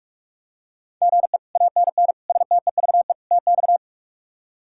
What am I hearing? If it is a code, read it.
Morse code, heard as a keyed tone, about 46 wpm, 700 Hz. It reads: GE ANN STEVE TX
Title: K1USN SST QSO Practice - 45wpm